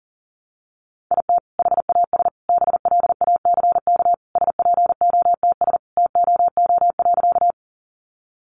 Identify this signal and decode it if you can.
Morse code, heard as a keyed tone, about 37 wpm, 700 Hz. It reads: IT HAS BLACK SPOTS TOO.
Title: Additional Easy Phrases 40wpm